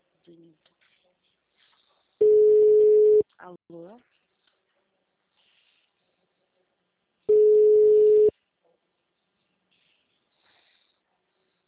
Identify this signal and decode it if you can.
Ringback tone (Europe, 425 Hz), with speech.